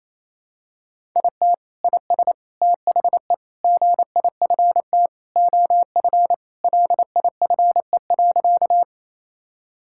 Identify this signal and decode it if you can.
Morse code, heard as a keyed tone, about 28 wpm, 700 Hz. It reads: IT IS THE GIFT OF LIFE.
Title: Sentences from Top 1000 Words - Review Part 6 - 28wpm